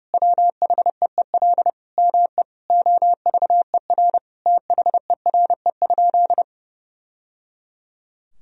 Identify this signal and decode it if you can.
Morse code, heard as a keyed tone, about 32 wpm, 700 Hz. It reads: WHEEL ME OVER THERE?
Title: Sentences from Top 1000 Words - Review Part 3 - 30wpm